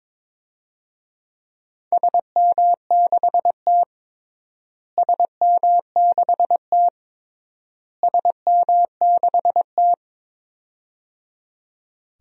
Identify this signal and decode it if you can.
Morse code, heard as a keyed tone, about 22 wpm, 700 Hz. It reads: SM6T SM6T SM6T
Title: DXCC-100 Call Signs - 4 Characters - Repeated 3x - 22wpm